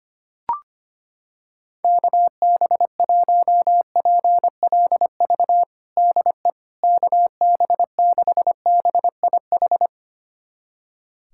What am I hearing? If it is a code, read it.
Morse code, heard as a keyed tone, about 25 wpm, 700 Hz. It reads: KB1PLV DE KB6BIH
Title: USA Callsign DE Callsign - 6 Characters - 25wpm